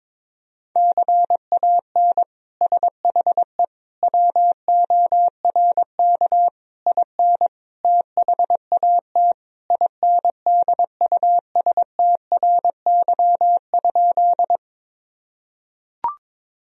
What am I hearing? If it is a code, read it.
Morse code, heard as a keyed tone, about 22 wpm, 700 Hz. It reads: CAN SHE WORK IN THAT INDUSTRY?
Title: Sentences from Top 1000 Words - Review Part 4 - 22wpm